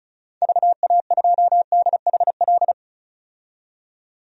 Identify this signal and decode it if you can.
Morse code, heard as a keyed tone, about 35 wpm, 700 Hz. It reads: VA2DHL
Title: DXCC-100 Call Signs - 6 Characters - 35wpm